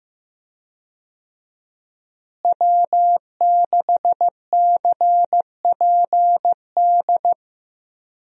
Morse code, heard as a keyed tone, about 15 wpm, 700 Hz. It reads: W6CPD
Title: USA Call Signs - 5 Characters - Repeated 3x - 15wpm